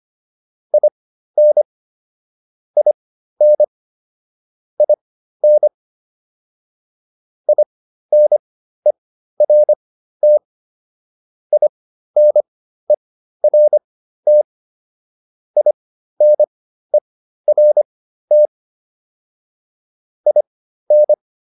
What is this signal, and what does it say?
Morse code, heard as a keyed tone, about 25 wpm, spaced as for 13 wpm, 600 Hz. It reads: IN IN IN INERT INERT INERT IN